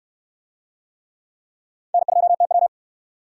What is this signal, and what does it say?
Morse code, heard as a keyed tone, about 69 wpm, 700 Hz. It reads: N2TW